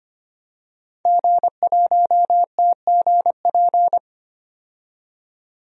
Morse code, heard as a keyed tone, about 25 wpm, 700 Hz. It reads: G1TGP
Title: DXCC-100 Call Signs - 5 Characters - 25wpm